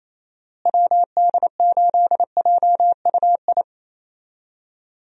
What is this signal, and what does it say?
Morse code, heard as a keyed tone, about 28 wpm, 700 Hz. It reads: WD8JUI
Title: USA Call Signs - 6 Characters - Repeated 3x - 28wpm